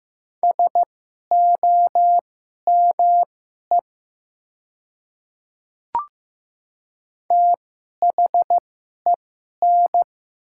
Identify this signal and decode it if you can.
Morse code, heard as a keyed tone, about 15 wpm, spaced as for 11 wpm, 700 Hz. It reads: SOME THEN
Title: 2x Character Spacing - Top 100 Words - 15wpm